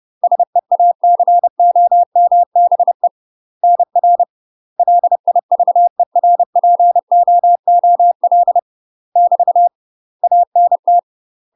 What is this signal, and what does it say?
Morse code, heard as a keyed tone, about 30 wpm, 700 Hz. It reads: SEACOMBE NR LIVERPOOL <BT> ANT
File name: UK MASTER QSOs TRAINING / 30WPM / UK MASTER QSOs 0000